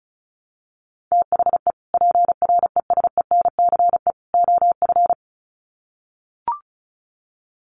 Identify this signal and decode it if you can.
Morse code, heard as a keyed tone, about 34 wpm, 700 Hz. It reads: THE PRESENCE OF
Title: Sets of 3 Words 35wpm